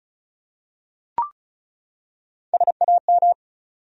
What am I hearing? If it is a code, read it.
Morse code, heard as a keyed tone, about 34 wpm, 700 Hz. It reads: SAM
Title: USA Common Names - Max 4 Letters - 35wpm